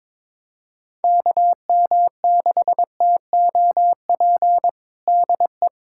Morse code, heard as a keyed tone, about 22 wpm, 700 Hz. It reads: KM6TOP DE
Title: USA Callsign DE Callsign - 6 Characters - 22wpm